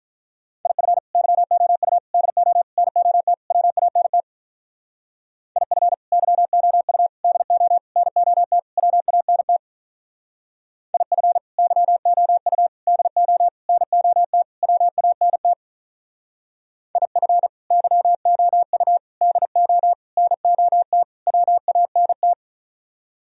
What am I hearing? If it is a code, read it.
Morse code, heard as a keyed tone, about 44 wpm, 700 Hz. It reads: IF YOU DO NOT WANT IF YOU DO NOT WANT IF YOU DO NOT WANT IF YOU DO NOT WANT
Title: Speed-Racing - Sets of 5 Words 35wpm